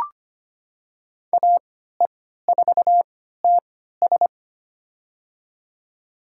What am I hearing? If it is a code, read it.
Morse code, heard as a keyed tone, about 25 wpm, spaced as for 14 wpm, 700 Hz. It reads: AE4TS
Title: USA Call Signs - 3x Character Spacing - 25wpm